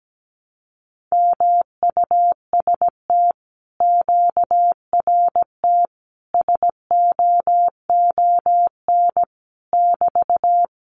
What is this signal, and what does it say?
Morse code, heard as a keyed tone, about 17 wpm, 700 Hz. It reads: MUST QRT SOON <BT>
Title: Common QSO Phrases Encore 17wpm